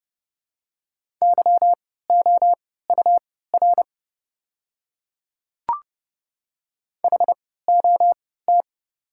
Morse code, heard as a keyed tone, about 31 wpm, spaced as for 17 wpm, 690 Hz. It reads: YOUR HOT
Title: 3x Character Spacing - Top 100 Words - 30wpm